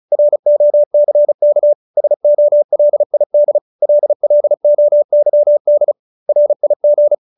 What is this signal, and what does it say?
Morse code, heard as a keyed tone, about 34 wpm, 570 Hz. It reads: ROCK SOLID LLOYD RIG